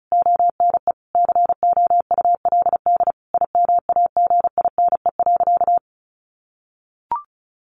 Morse code, heard as a keyed tone, about 34 wpm, 700 Hz. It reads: ONE COULD IMAGINE.
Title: Sentences from Top 1400 Words 35wpm